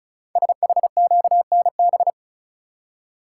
Morse code, heard as a keyed tone, about 36 wpm, 690 Hz. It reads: SHQNB
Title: Code Groups - 5 Letters - 35wpm